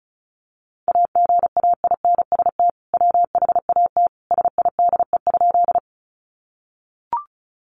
Morse code, heard as a keyed tone, about 34 wpm, 700 Hz. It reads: AGAINST WHAT SIDE?
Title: Sentences from Top 400 Words 35wpm